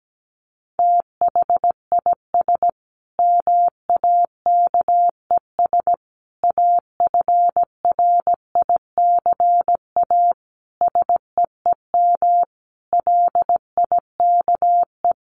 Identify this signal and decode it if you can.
Morse code, heard as a keyed tone, about 17 wpm, 700 Hz. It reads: THIS MAKES AFRICA SEEM LIKE
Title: A Sound Of Thunder 17wpm